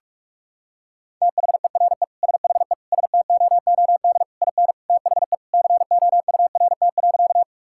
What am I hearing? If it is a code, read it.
Morse code, heard as a keyed tone, about 46 wpm, 700 Hz. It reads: THERE SHE STOOD IN THE COURT.
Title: Sentences from Top 1600 Words 45wpm